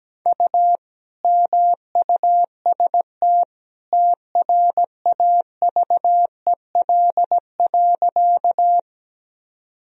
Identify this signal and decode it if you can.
Morse code, heard as a keyed tone, about 17 wpm, 700 Hz. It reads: U MUST TRAVEL.
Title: Sentences from Top 400 Words and Abbreviations 17wpm